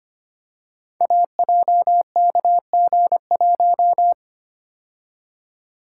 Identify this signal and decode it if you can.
Morse code, heard as a keyed tone, about 25 wpm, 700 Hz. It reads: AJKG1
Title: Code Groups - 5 Letter:Numbers - 25wpm